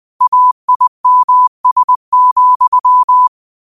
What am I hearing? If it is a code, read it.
Morse code, heard as a keyed tone, about 20 wpm, 1000 Hz. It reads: AIMS,
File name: qotdfiles / 20wpm